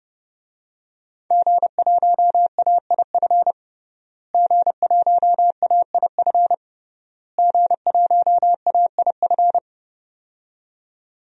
Morse code, heard as a keyed tone, about 32 wpm, 700 Hz. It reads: G1AIF G1AIF G1AIF
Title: DXCC-100 Call Signs - 5 Characters - Repeated 3x - 30wpm